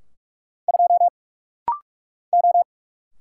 Morse code, heard as a keyed tone, about 44 wpm, 700 Hz. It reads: 2 O
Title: Single Letter-Number - Rapid-Fire 45wpm